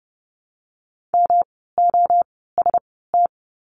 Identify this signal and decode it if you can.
Morse code, heard as a keyed tone, about 30 wpm, spaced as for 17 wpm, 700 Hz. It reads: MOST
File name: course / 30:10wpm / 108 Lesson 18f - Letter m - Words 30wpm